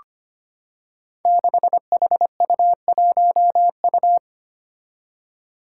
Morse code, heard as a keyed tone, about 25 wpm, 700 Hz. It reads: 6HU1U
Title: Five Challenging Letter-Number Combinations 25wpm